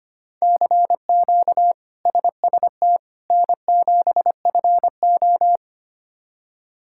Morse code, heard as a keyed tone, about 25 wpm, 700 Hz. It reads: CQ SST N7FO
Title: K1USN SST QSO Practice - 25wpm